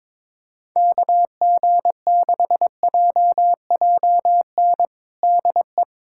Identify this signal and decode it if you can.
Morse code, heard as a keyed tone, about 22 wpm, 700 Hz. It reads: KG6JJN DE
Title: USA Callsign DE Callsign - 6 Characters - 22wpm